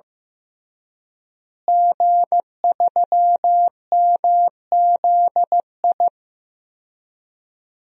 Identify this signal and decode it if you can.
Morse code, heard as a keyed tone, about 16 wpm, 700 Hz. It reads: G3MZI
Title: DXCC-100 Call Signs - 5 Characters - Repeated 3x - 15wpm